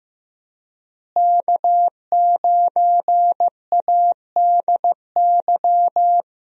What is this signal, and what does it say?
Morse code, heard as a keyed tone, about 16 wpm, 700 Hz. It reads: K9ADY
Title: USA Call Signs - 5 Characters - Repeated 3x - 15wpm